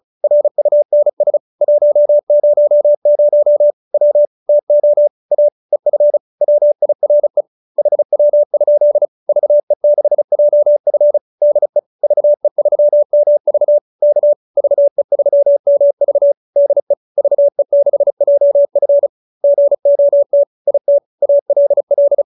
Morse code, heard as a keyed tone, about 34 wpm, 590 Hz. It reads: RUNS 100 W TO A EF WIRE HW? VE6JF DE VE3MV K VE3MV DE VE6JF GOT IT ALL